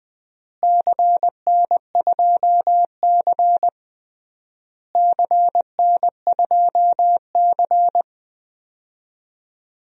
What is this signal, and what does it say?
Morse code, heard as a keyed tone, about 19 wpm, 700 Hz. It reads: CN2C CN2C
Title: DXCC-100 Call Signs - 4 Characters - Repeated 3x - 20wpm